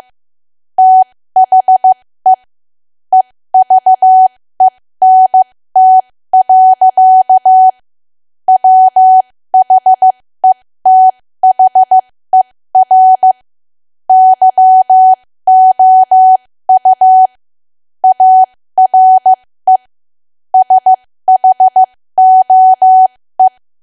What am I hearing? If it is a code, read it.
Morse code, heard as a keyed tone, about 15 wpm, spaced as for 13 wpm, 750 Hz. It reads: THE EVENT. WHETHER YOU ARE SHOE
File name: cwa / W1AW-Level2 / 13WPM